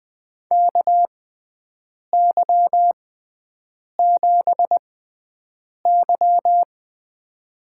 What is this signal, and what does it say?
Morse code, heard as a keyed tone, about 21 wpm, spaced as for 7 wpm, 700 Hz. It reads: KY7Y
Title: USA Call Signs - 4 Characters - 6x Character Spacing - 20wpm